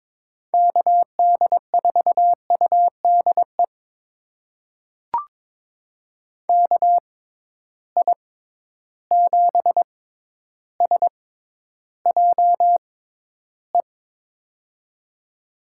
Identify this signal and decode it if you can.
Morse code, heard as a keyed tone, about 22 wpm, spaced as for 11 wpm, 700 Hz. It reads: KD4UDE KI7SJE
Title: USA Call Signs - 6 Characters - 6x Character Spacing - 22wpm